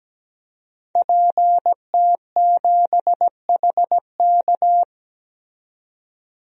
Morse code, heard as a keyed tone, about 17 wpm, 700 Hz. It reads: PT7HK
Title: DXCC-100 Call Signs - 5 Characters - 17wpm